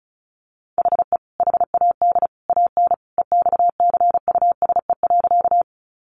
Morse code, heard as a keyed tone, about 34 wpm, 700 Hz. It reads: HE HAD AN EXCUSE.
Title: Sentences from Top 800 Words 35wpm